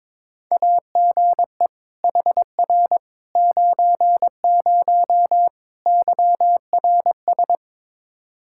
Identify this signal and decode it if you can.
Morse code, heard as a keyed tone, about 22 wpm, 700 Hz. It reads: AGE HR 90 YRS
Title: K1USN SST QSO Practice 22wpm